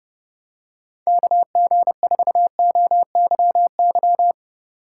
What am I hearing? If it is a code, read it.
Morse code, heard as a keyed tone, about 31 wpm, 700 Hz. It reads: KG4OYY